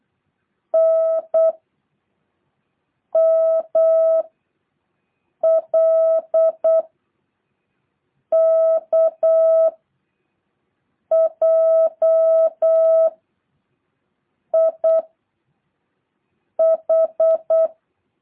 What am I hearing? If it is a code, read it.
Morse code, heard as a keyed tone, about 8 wpm, 650 Hz. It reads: N M L K J I H
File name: jh8cbh